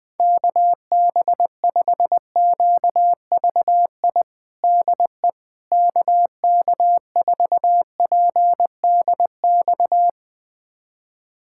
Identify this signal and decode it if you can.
Morse code, heard as a keyed tone, about 21 wpm, 700 Hz. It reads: KB5QVI DE KK4PDX
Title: USA Callsign DE Callsign - 6 Characters - 20wpm